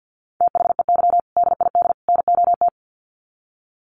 Morse code, heard as a keyed tone, about 50 wpm, 700 Hz. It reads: THEY DID NOT
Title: Sets of 3 Words 50wpm